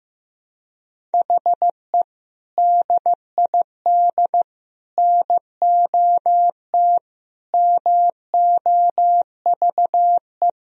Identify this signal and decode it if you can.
Morse code, heard as a keyed tone, about 15 wpm, 700 Hz. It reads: HE DID NOT MOVE
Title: Sentences from Top 1000 Words - Review Part 1 - 15wpm